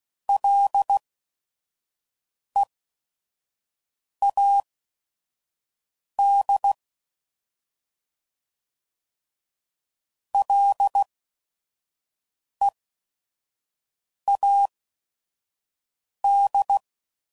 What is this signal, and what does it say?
Morse code, heard as a keyed tone, about 16 wpm, spaced as for 5 wpm, 800 Hz. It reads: LEAD LEAD
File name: Prefixes / FLs & THs / Single L's